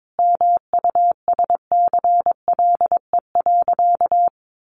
Morse code, heard as a keyed tone, about 22 wpm, 700 Hz. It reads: MUSCLE.
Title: Sentences from Top 1300 Words 22wpm